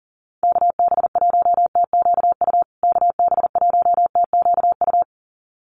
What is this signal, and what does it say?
Morse code, heard as a keyed tone, about 40 wpm, 700 Hz. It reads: KB1TQU KB1TQU
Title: USA Call Signs Repeated 2x 40wpm